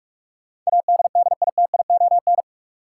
Morse code, heard as a keyed tone, about 45 wpm, 700 Hz. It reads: ADDITION
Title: Sets of 3 Words - New Word Drill - 45wpm